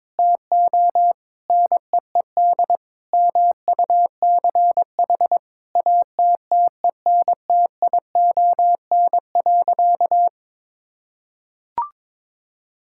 Morse code, heard as a keyed tone, about 21 wpm, 700 Hz. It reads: TO NEED MUCH ATTENTION.
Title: Sentences from Top 1000 Words - Review Part 6 - 22wpm